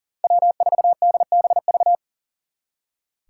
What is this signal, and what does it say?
Morse code, heard as a keyed tone, about 52 wpm, 700 Hz. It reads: W4DBV